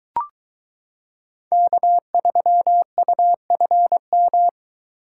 Morse code, heard as a keyed tone, about 23 wpm, 700 Hz. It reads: K3UFM